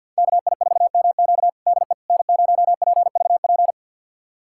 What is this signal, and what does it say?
Morse code, heard as a keyed tone, about 52 wpm, 700 Hz. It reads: KI4MQ DE N0PVP
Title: USA Callsign DE Callsign - 5 Characters - 50wpm